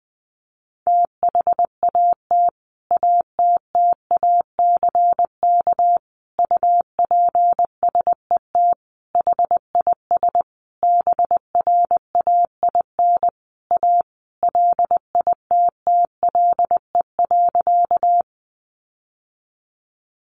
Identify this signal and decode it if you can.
Morse code, heard as a keyed tone, about 20 wpm, 700 Hz. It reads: THAT ATTACK UPSET HIS BRAIN A LITTLE.
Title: Sentences from Top 1500 Words 20wpm